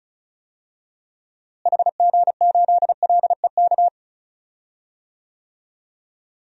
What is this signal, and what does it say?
Morse code, heard as a keyed tone, about 34 wpm, 700 Hz. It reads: HG8LEK
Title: DXCC-100 Call Signs - 6 Characters - Repeated 2x - 35wpm